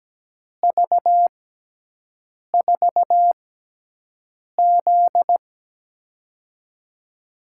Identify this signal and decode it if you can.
Morse code, heard as a keyed tone, about 17 wpm, spaced as for 6 wpm, 700 Hz. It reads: V4Z
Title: USA Call Signs - 4 Characters - 6x Character Spacing - 17wpm